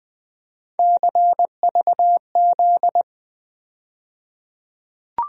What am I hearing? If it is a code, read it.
Morse code, heard as a keyed tone, about 20 wpm, 700 Hz. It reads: CVZ
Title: USA Call Sign Postfixes - 3 Letters 20wpm